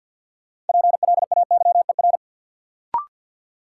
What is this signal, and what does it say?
Morse code, heard as a keyed tone, about 52 wpm, 700 Hz. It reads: PLAYER
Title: Sentences from Top 1000 Words - Review Part 4 - New Word Drill - 50wpm